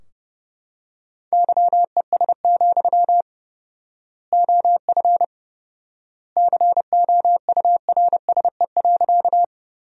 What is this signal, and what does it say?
Morse code, heard as a keyed tone, about 30 wpm, 700 Hz. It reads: YES, OF COURSE.
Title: 4x Word-Spacing - Sentences from Top 500 Words - 30wpm